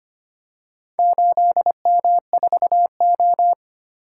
Morse code, heard as a keyed tone, about 25 wpm, 700 Hz. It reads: 8M4O
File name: practice / 25wpm / Four Letter-Number Combinations - 25wpm